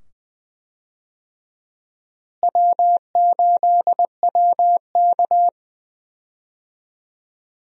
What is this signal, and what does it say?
Morse code, heard as a keyed tone, about 20 wpm, 700 Hz. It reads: W8WK
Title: USA Call Signs - 4 Characters - Repeated 2x - 20wpm